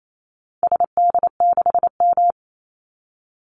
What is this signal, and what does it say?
Morse code, heard as a keyed tone, about 28 wpm, 700 Hz. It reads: SD6M
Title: Call Signs - 28wpm